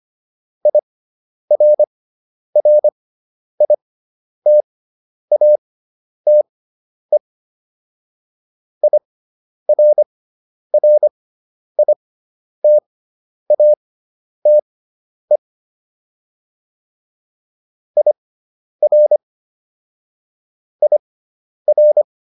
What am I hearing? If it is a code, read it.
Morse code, heard as a keyed tone, about 25 wpm, spaced as for 10 wpm, 600 Hz. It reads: IRRITATE IRRITATE IR IR